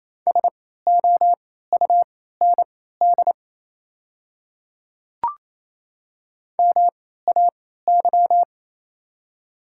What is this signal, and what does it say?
Morse code, heard as a keyed tone, about 27 wpm, spaced as for 16 wpm, 700 Hz. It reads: SOUND MAY
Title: 3x Character Spacing - Top 100 Words - 28wpm